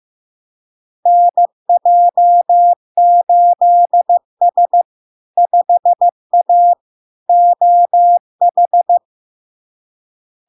Morse code, heard as a keyed tone, about 15 wpm, 700 Hz. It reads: NJ8S 5A OH
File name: LARK-FD-2022-Log-in-CW-15wpm-700-Hz